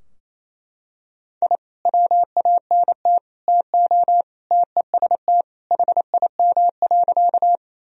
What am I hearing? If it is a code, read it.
Morse code, heard as a keyed tone, about 27 wpm, 700 Hz. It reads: I WANT TO TEST HIM.
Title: Sentences from Top 1000 Words - Review Part 3 - 28wpm